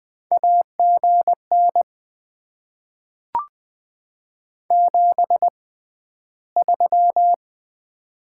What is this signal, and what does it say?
Morse code, heard as a keyed tone, about 20 wpm, 700 Hz. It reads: A G N 7 3
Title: Top 20 QSO Elements - 6x Character Spacing - 20wpm